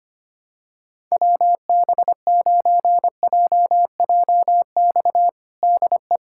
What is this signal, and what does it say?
Morse code, heard as a keyed tone, about 25 wpm, 700 Hz. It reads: WB9JJX DE
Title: USA Callsign DE Callsign - 6 Characters - 25wpm